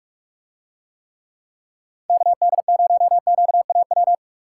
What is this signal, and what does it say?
Morse code, heard as a keyed tone, about 46 wpm, 700 Hz. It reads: KD0QAW